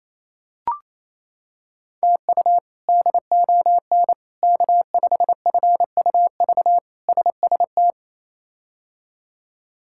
Morse code, heard as a keyed tone, about 28 wpm, 700 Hz. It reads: TU DON K5FUV SST
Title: K1USN SST QSO Practice - 28wpm